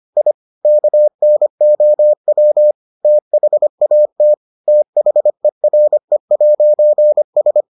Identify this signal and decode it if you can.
Morse code, heard as a keyed tone, about 25 wpm, 600 Hz. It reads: I KNOW THAT THERE'S